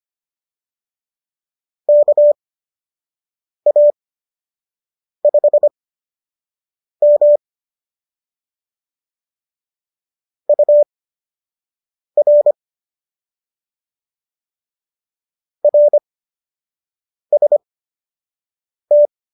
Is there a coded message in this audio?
Yes: Morse code from 1.9 to 19.1 s, heard as a keyed tone, about 25 wpm, spaced as for 6 wpm, 590 Hz. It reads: KA5M UR RST